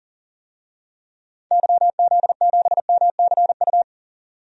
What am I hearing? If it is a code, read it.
Morse code, heard as a keyed tone, about 38 wpm, 700 Hz. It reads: YZ7MCU
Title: DXCC-100 Call Signs - 6 Characters - 40wpm